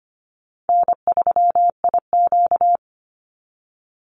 Morse code, heard as a keyed tone, about 25 wpm, 700 Hz. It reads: N3IQ
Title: Call Signs Repeated 2x - 25wpm